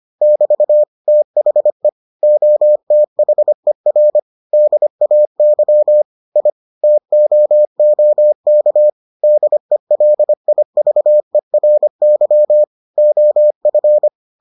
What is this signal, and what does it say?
Morse code, heard as a keyed tone, about 25 wpm, 600 Hz. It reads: <BT> THE OTHER DAY I TOOK DELIVERY OF